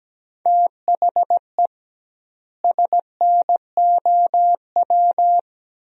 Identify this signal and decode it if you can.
Morse code, heard as a keyed tone, about 17 wpm, 700 Hz. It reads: THE SNOW
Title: 2x Word-Spacing - Sentences from Top 500 Words - 17wpm